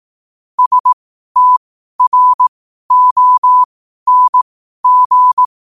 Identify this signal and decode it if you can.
Morse code, heard as a keyed tone, about 18 wpm, spaced as for 12 wpm, 1000 Hz. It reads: STRONG
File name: qotdfiles / 12wpm